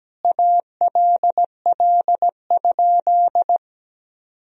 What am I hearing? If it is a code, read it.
Morse code, heard as a keyed tone, about 17 wpm, 700 Hz. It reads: ALL?
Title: Sentences from Top 1000 Words - Review Part 2 - 17wpm